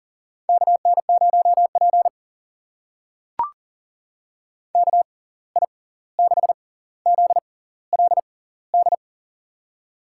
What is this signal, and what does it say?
Morse code, heard as a keyed tone, about 40 wpm, spaced as for 18 wpm, 700 Hz. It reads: KN0P KI6ZLD
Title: USA Call Signs - 6x Character Spacing - 40wpm